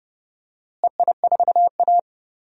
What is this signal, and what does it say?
Morse code, heard as a keyed tone, about 30 wpm, 700 Hz. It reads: EI4A
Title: DXCC-100 Call Signs - 4 Characters - Repeated 3x - 30wpm